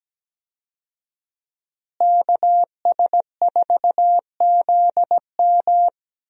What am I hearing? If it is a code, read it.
Morse code, heard as a keyed tone, about 17 wpm, 700 Hz. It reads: KS4ZM
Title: USA Call Signs - 5 Characters - Repeated 2x - 17wpm